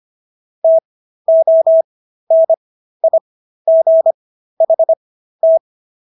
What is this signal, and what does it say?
Morse code, heard as a keyed tone, about 25 wpm, spaced as for 13 wpm, 650 Hz. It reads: TONIGHT